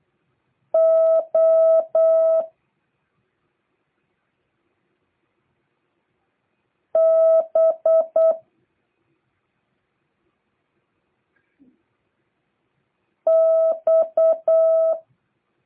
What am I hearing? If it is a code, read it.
Morse code, heard as a keyed tone, about 8 wpm, 650 Hz. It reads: O B X